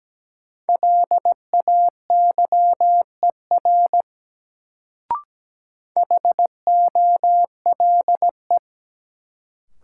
Morse code, heard as a keyed tone, about 17 wpm, 700 Hz. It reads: LAYER HOLE
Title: Sentences from Top 1600 Words - New Word Drill Alt - 17wpm